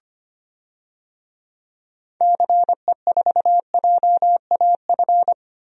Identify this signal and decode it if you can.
Morse code, heard as a keyed tone, about 25 wpm, 700 Hz. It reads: CE4JAF
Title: DXCC-100 Call Signs - 6 Characters - 25wpm